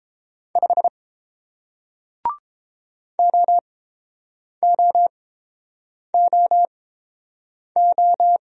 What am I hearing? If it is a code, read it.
Morse code, heard as a keyed tone, about 30 wpm, spaced as for 7 wpm, 700 Hz. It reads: 5 OOOO